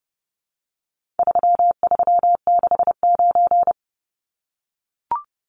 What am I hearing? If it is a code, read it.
Morse code, heard as a keyed tone, about 30 wpm, 700 Hz. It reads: 3369
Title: Numbers - 4-Digit 30wpm